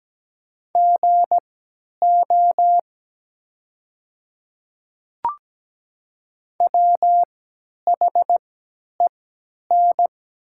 Morse code, heard as a keyed tone, about 17 wpm, spaced as for 10 wpm, 700 Hz. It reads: GO WHEN
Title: 3x Character Spacing - Top 100 Words - 17wpm